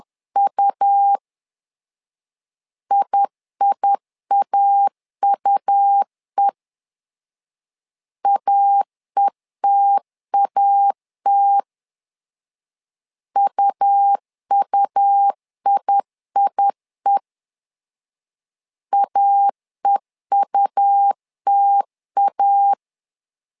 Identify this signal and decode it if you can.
Morse code, heard as a keyed tone, about 10 wpm, 800 Hz. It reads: U IIAUE AETAT UUIIE AEUTA
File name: LEZIONI / 2 UHDN / 2 UH 10wpm 1714